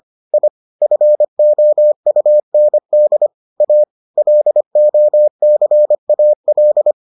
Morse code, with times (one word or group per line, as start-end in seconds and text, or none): 0.3-0.5 I
0.8-3.3 FOUND
3.6-3.8 A
4.2-6.9 LOCAL